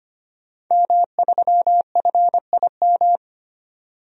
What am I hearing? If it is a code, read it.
Morse code, heard as a keyed tone, about 25 wpm, 700 Hz. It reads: M3FIM
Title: DXCC-100 Call Signs - 5 Characters - 25wpm